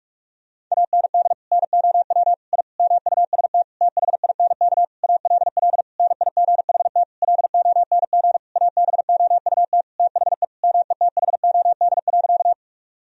Morse code, heard as a keyed tone, about 42 wpm, 700 Hz. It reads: AND NOW I MUST THINK ALL NIGHT LONG ABOUT THE METHOD.
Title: Sentences from Top 1000 Words - Review Part 3 - 45wpm